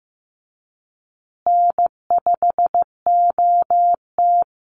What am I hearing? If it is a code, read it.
Morse code, heard as a keyed tone, about 15 wpm, 700 Hz. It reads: N5OT
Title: Easy Call Signs Repeated 2x - 15wpm